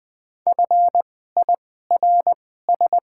Morse code, heard as a keyed tone, about 20 wpm, spaced as for 15 wpm, 700 Hz. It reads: FIRS
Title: 2x Character Spacing - Top 100 Words - 20wpm